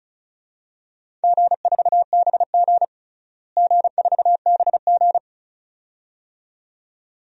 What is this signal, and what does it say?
Morse code, heard as a keyed tone, about 36 wpm, 700 Hz. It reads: G4BG G4BG
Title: DXCC-100 Call Signs - 4 Characters - Repeated 2x - 35wpm